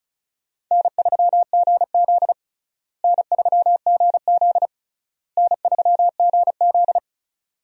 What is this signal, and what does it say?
Morse code, heard as a keyed tone, about 36 wpm, 700 Hz. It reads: N3GZ N3GZ N3GZ
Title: USA Call Signs - 4 Characters - Repeated 3x - 35wpm